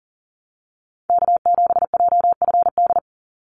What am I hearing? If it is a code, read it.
Morse code, heard as a keyed tone, about 40 wpm, 700 Hz. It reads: K7JFD
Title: NA SOTA Activator Callsigns 40wpm